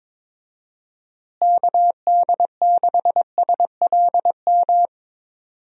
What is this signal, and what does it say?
Morse code, heard as a keyed tone, about 22 wpm, 700 Hz. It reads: KD6SLM
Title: USA Call Signs - 6 Characters - Repeated 2x - 22wpm